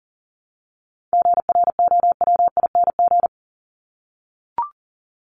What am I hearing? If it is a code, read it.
Morse code, heard as a keyed tone, about 40 wpm, 700 Hz. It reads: GROWING
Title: ING Suffix - Top 100 Words with Base Words 40wpm